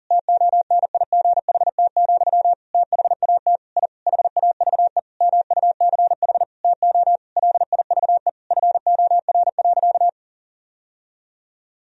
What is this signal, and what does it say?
Morse code, heard as a keyed tone, about 40 wpm, 700 Hz. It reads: TONIGHT, THAT I HAVE MUCH TO LIVE FOR.
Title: Sentences from Top 1000 Words - Review Part 3 - 40wpm